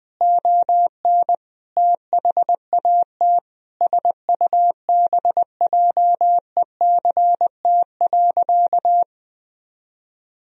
Morse code, heard as a keyed tone, about 20 wpm, 700 Hz. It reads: ON THAT SUBJECT.
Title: Sentences from Top 1000 Words - Review Part 4 - 20wpm